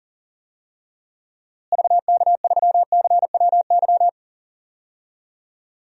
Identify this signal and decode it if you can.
Morse code, heard as a keyed tone, about 40 wpm, 700 Hz. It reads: VK3CWY
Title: DXCC-100 Call Signs - 6 Characters - Repeated 2x - 40wpm